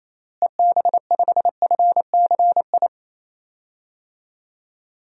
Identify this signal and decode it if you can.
Morse code, heard as a keyed tone, about 28 wpm, 700 Hz. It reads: EB5FCI
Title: DXCC-100 Call Signs - 6 Characters - Repeated 3x - 28wpm